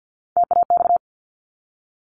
Morse code, heard as a keyed tone, about 50 wpm, 700 Hz. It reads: TAX
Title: Top 501-750 Words 50wpm